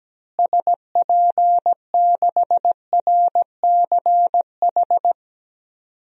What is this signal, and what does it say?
Morse code, heard as a keyed tone, about 17 wpm, 700 Hz. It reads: SP6RCH
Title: DXCC-100 Call Signs - 6 Characters - Repeated 3x - 17wpm